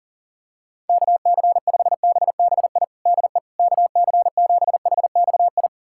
Morse code, heard as a keyed tone, about 40 wpm, 700 Hz. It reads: KC5BBI DE KC7HXI
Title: USA Callsign DE Callsign - 6 Characters - 40wpm